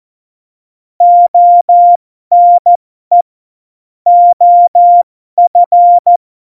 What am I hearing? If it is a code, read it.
Morse code, heard as a keyed tone, about 14 wpm, spaced as for 12 wpm, 700 Hz. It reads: ONE OF
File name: Morsekurs / Witze